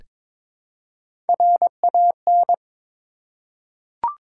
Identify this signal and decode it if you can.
Morse code, heard as a keyed tone, about 22 wpm, 700 Hz. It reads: RAN
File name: practice / 22wpm / Three Letter-Number Combinations - 22wpm